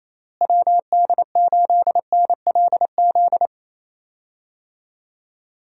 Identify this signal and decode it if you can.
Morse code, heard as a keyed tone, about 28 wpm, 700 Hz. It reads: WD8NLZ
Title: USA Call Signs - 6 Characters - Repeated 2x - 28wpm